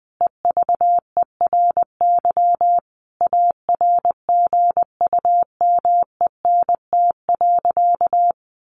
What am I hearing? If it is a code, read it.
Morse code, heard as a keyed tone, about 20 wpm, 700 Hz. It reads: EVERY ARGUMENT.
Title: Sentences from Top 1400 Words 20wpm